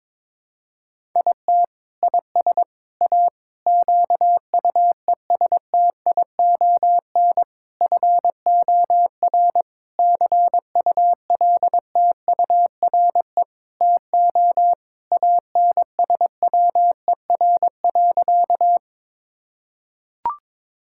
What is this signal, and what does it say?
Morse code, heard as a keyed tone, about 22 wpm, 700 Hz. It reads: IT IS A QUESTION FOR CULTURE TO ANSWER.
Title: Sentences from Top 1000 Words - Review Part 6 - 22wpm